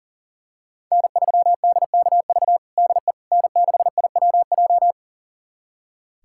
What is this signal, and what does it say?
Morse code, heard as a keyed tone, about 37 wpm, 700 Hz. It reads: N3DKV DE N6IWJ
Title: USA Callsign DE Callsign - 5 Characters - 40wpm